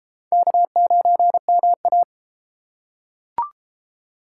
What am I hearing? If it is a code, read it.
Morse code, heard as a keyed tone, about 31 wpm, 700 Hz. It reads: K9MA